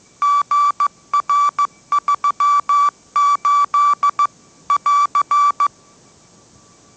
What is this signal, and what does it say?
Morse code, heard as a keyed tone, about 15 wpm, 1160 Hz. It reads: GR38 <AR>
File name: mrx / mrx066 16wpm